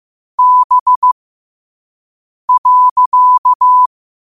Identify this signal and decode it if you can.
Morse code, heard as a keyed tone, about 15 wpm, spaced as for 5 wpm, 1000 Hz. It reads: B.
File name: qotdfiles / 5wpm